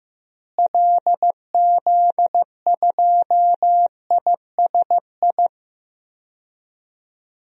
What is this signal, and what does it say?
Morse code, heard as a keyed tone, about 15 wpm, 700 Hz. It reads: LZ2ISI
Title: DXCC-100 Call Signs - 6 Characters - Repeated 3x - 15wpm